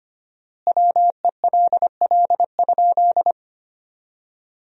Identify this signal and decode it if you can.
Morse code, heard as a keyed tone, about 25 wpm, 700 Hz. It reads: WELL?
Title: 4x Word-Spacing - Sentences from Top 400 Words - 25wpm